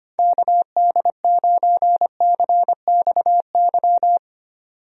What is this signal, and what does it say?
Morse code, heard as a keyed tone, about 25 wpm, 700 Hz. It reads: KD9CXY
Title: USA Call Signs - 6 Characters - 25wpm